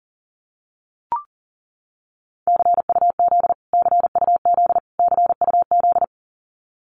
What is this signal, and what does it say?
Morse code, heard as a keyed tone, about 40 wpm, 700 Hz. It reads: CUZ CUZ CUZ
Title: Abbreviations Repeated 3x - 40wpm